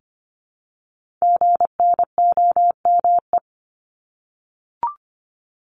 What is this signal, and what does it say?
Morse code, heard as a keyed tone, about 25 wpm, 700 Hz. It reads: GNOME
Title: Multi-letter Phonemes Repeated 3x 25wpm